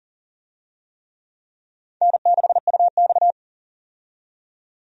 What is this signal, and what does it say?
Morse code, heard as a keyed tone, about 37 wpm, 700 Hz. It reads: N6UX
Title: USA Call Signs - 4 Characters - 40wpm